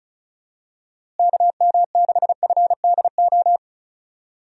Morse code, heard as a keyed tone, about 36 wpm, 700 Hz. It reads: KM6FDO
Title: USA Call Signs - 6 Characters - Repeated 2x - 35wpm